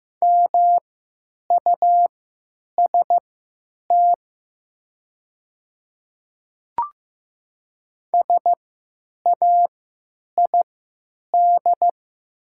Morse code, heard as a keyed tone, about 16 wpm, spaced as for 9 wpm, 700 Hz. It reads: MUST SAID
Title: 3x Character Spacing - Top 300 Words - 15wpm